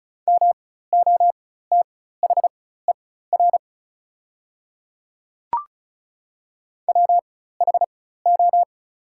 Morse code, heard as a keyed tone, about 36 wpm, spaced as for 16 wpm, 700 Hz. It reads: MOTHER WHO
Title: 4x Character Spacing - Top 200 Words - 35wpm